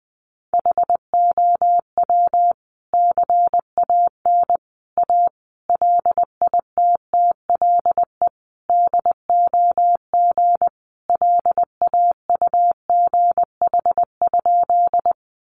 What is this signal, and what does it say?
Morse code, heard as a keyed tone, about 20 wpm, 700 Hz. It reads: HOW CAN A LITTLE DOG LAUGH?
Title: Sentences from Top 500 Words 20wpm